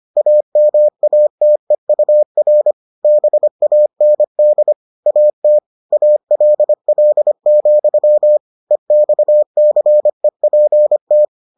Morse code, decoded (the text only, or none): AMATEUR BAND AT ALL, EXCEPT